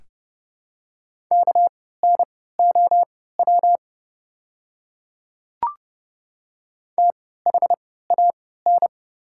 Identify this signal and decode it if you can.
Morse code, heard as a keyed tone, about 30 wpm, spaced as for 17 wpm, 700 Hz. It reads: KNOW THAN
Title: 3x Character Spacing - Top 100 Words - 30wpm